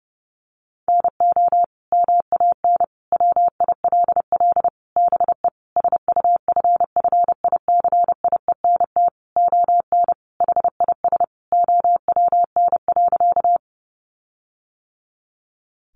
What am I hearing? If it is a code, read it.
Morse code, heard as a keyed tone, about 30 wpm, 700 Hz. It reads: NO MAN WILL BE SUFFICIENT ON HIS OWN.
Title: Sentences from Top 1400 Words 30wpm